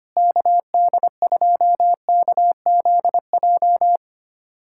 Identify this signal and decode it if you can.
Morse code, heard as a keyed tone, about 25 wpm, 700 Hz. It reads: KD2KZJ
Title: USA Callsign DE Callsign 25wpm